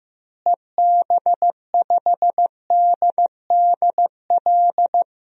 Morse code, heard as a keyed tone, about 15 wpm, 700 Hz. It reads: EB5DDL
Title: DXCC-100 Call Signs - 6 Characters - Repeated 2x - 15wpm